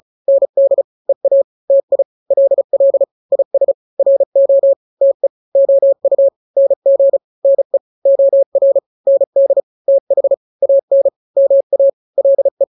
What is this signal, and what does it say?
Morse code, heard as a keyed tone, about 34 wpm, 550 Hz. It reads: ND EA TI LL IS RO TE OU NG NE OR ND TH AN MA LE
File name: n-grams / 2-grams 35 30 WPM